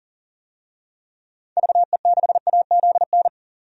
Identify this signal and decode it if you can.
Morse code, heard as a keyed tone, about 40 wpm, 700 Hz. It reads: VE6AZN
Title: DXCC-100 Call Signs - 6 Characters - 40wpm